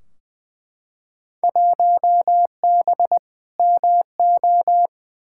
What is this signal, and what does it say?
Morse code, heard as a keyed tone, about 20 wpm, 700 Hz. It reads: 1B MO
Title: Field Day Exchanges - 20wpm